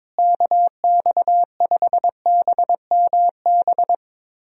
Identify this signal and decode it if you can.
Morse code, heard as a keyed tone, about 22 wpm, 700 Hz. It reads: KX5BMB
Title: USA Call Signs - 6 Characters - Repeated 2x - 22wpm